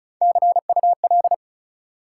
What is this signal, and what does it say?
Morse code, heard as a keyed tone, about 34 wpm, 700 Hz. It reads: CUL
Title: Top 180 QSO Elements 35wpm